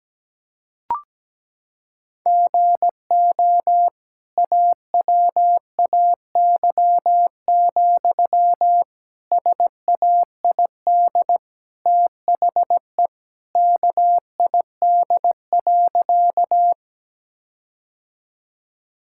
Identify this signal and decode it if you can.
Morse code, heard as a keyed tone, about 17 wpm, 700 Hz. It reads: GO AWAY, SAID THE KID.
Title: Sentences from Top 1600 Words 17wpm